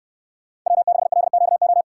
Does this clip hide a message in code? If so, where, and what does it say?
Yes: Morse code from 0.7 to 1.8 s, heard as a keyed tone, about 68 wpm, 700 Hz. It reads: W6LQO